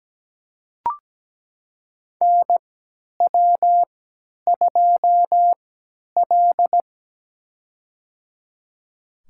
Morse code, heard as a keyed tone, about 17 wpm, spaced as for 10 wpm, 700 Hz. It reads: NW2L
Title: USA Call Signs - 3x Character Spacing - 17wpm